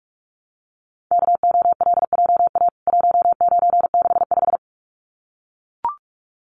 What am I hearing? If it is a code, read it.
Morse code, heard as a keyed tone, about 44 wpm, 700 Hz. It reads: KOLJA 1965
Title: CWT Exchange 45wpm